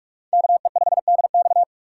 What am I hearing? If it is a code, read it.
Morse code, heard as a keyed tone, about 44 wpm, 700 Hz. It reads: KE5DX